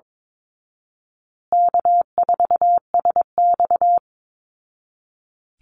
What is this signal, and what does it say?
Morse code, heard as a keyed tone, about 22 wpm, 700 Hz. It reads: K4SX
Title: Call Signs Repeated 2x - 22wpm